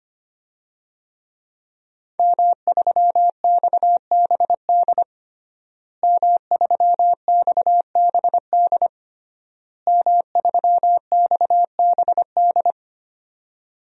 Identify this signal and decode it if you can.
Morse code, heard as a keyed tone, about 25 wpm, 700 Hz. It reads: M3XBD M3XBD M3XBD
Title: DXCC-100 Call Signs - 5 Characters - Repeated 3x - 25wpm